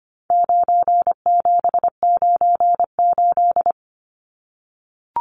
Morse code, heard as a keyed tone, about 25 wpm, 700 Hz. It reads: 9798
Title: Numbers - 4-Digit 25wpm